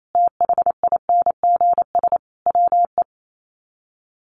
Morse code, heard as a keyed tone, about 28 wpm, 700 Hz. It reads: THINGS WE
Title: Sets of 5 Words Encore 28wpm